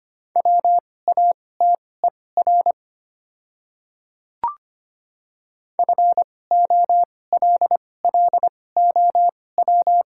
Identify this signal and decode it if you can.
Morse code, heard as a keyed tone, about 25 wpm, spaced as for 18 wpm, 700 Hz. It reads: WATER FOLLOW
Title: 2x Character Spacing - Top 200 Words - 25wpm